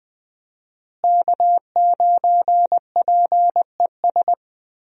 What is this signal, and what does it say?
Morse code, heard as a keyed tone, about 20 wpm, 700 Hz. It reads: K9PES
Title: USA Call Signs - 5 Characters - Repeated 2x - 20wpm